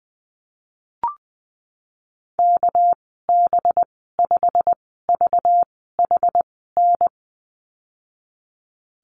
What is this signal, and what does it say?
Morse code, heard as a keyed tone, about 20 wpm, spaced as for 15 wpm, 700 Hz. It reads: KB5VHN
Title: 231 Lesson 37g - Letter x - Call Signs 20wpm